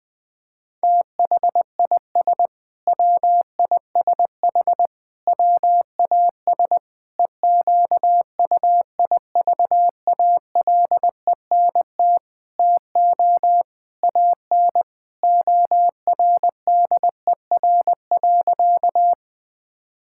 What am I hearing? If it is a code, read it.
Morse code, heard as a keyed tone, about 20 wpm, 700 Hz. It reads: THIS WISH WAS EQUIVALENT TO AN ORDER.
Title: Sentences from Top 1600 Words 20wpm